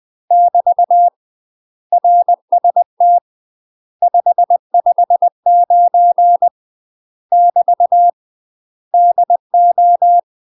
Morse code, heard as a keyed tone, about 20 wpm, 700 Hz. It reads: <BT> RST 559 <BT> DO
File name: UK MASTER QSOs TRAINING / 20WPM / UK MASTER QSOs 0012